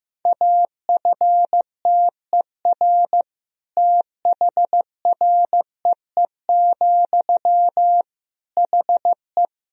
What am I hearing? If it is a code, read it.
Morse code, heard as a keyed tone, about 16 wpm, 700 Hz. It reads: AFTER THREE, HE
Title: Sentences from Top 200 Words and Abbreviations 15wpm